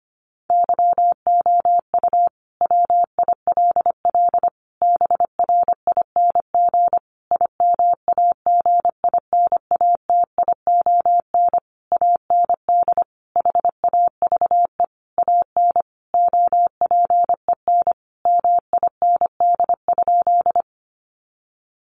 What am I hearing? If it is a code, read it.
Morse code, heard as a keyed tone, about 25 wpm, 700 Hz. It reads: YOU WILL BRING IMAGINATION AND HAVE AN OPEN MIND?
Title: Sentences from Top 1300 Words 25wpm